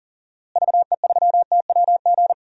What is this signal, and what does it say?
Morse code, heard as a keyed tone, about 39 wpm, 700 Hz. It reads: VE3TWG
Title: DXCC-100 Call Signs - 6 Characters - Repeated 3x - 40wpm